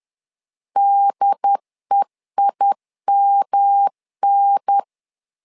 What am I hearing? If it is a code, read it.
Morse code, heard as a keyed tone, about 10 wpm, 800 Hz. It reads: DEIMN
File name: LEZIONI / 3 MWRO / 3 M 10wpm 1828